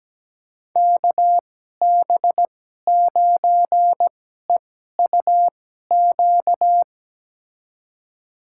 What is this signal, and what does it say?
Morse code, heard as a keyed tone, about 17 wpm, spaced as for 12 wpm, 700 Hz. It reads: KB9EUQ
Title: USA Call Signs - 6 Characters - 2x Character Spacing - 17wpm